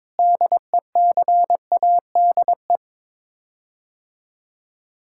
Morse code, heard as a keyed tone, about 22 wpm, 700 Hz. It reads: DECADE
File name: practice / 22wpm / Sets of 7 Words - 22wpm